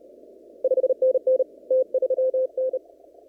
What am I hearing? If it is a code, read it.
Morse code, heard as a keyed tone, about 32 wpm, 520 Hz. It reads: 5NN T3N
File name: CW / kt5j wpx cw-op-k5pi